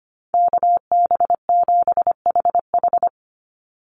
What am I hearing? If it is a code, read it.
Morse code, heard as a keyed tone, about 25 wpm, 700 Hz. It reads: KB7HH
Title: NA SOTA Chaser Callsigns 25wpm